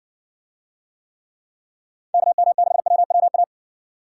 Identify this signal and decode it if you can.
Morse code, heard as a keyed tone, about 60 wpm, 700 Hz. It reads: KM6WWA